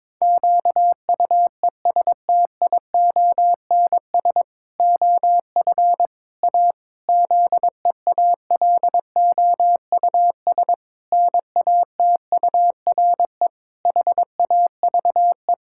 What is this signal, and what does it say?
Morse code, heard as a keyed tone, about 22 wpm, 700 Hz. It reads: QUESTIONS OF A ZEALOUS NATURE HAVE
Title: Pangram Challenge v2 - 22wpm